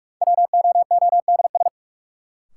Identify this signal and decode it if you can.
Morse code, heard as a keyed tone, about 46 wpm, 700 Hz. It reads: WOODS
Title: Sentences from Top 1000 Words - Review Part 2 - New Word Drill - 45wpm